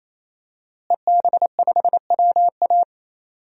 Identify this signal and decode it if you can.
Morse code, heard as a keyed tone, about 28 wpm, 700 Hz. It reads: EB5WA
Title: DXCC-100 Call Signs - 5 Characters - Repeated 3x - 28wpm